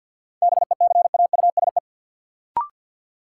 Morse code, heard as a keyed tone, about 52 wpm, 700 Hz. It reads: BECAUSE
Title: Sentences from Top 600 Words - New Word Drill - 50wpm